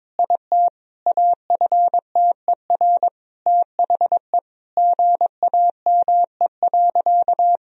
Morse code, heard as a keyed tone, about 22 wpm, 700 Hz. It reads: IT AFTER THE GAME.
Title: Sentences from Top 1000 Words - Review Part 4 - 22wpm